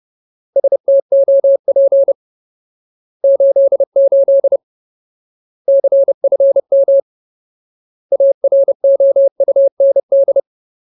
Morse code, heard as a keyed tone, about 30 wpm, 550 Hz. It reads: STOP 88 CFM AROUND
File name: Ultimate Word List / 30WPM Ultimate